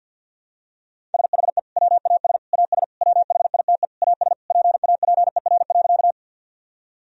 Morse code, heard as a keyed tone, about 51 wpm, 700 Hz. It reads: SHE WAS AS WHITE AS PAPER.
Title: Sentences from Top 1000 Words - Review Part 1 - 50wpm